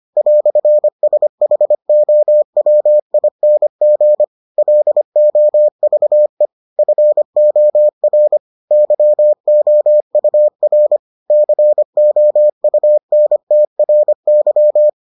Morse code, heard as a keyed tone, about 25 wpm, 600 Hz. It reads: "SHOWING LOVE FOR YOUR COUNTRY